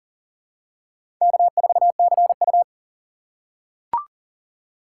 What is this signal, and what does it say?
Morse code, heard as a keyed tone, about 37 wpm, 700 Hz. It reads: K4CU
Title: USA Call Signs - 6x Character Spacing - 40wpm